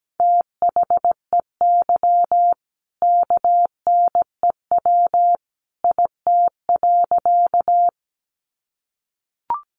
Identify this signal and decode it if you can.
Morse code, heard as a keyed tone, about 17 wpm, 700 Hz. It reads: THEY KNEW IT.
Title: Sentences from Top 400 Words 17wpm